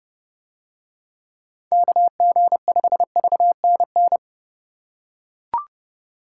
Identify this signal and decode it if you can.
Morse code, heard as a keyed tone, about 28 wpm, 700 Hz. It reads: KG5VNN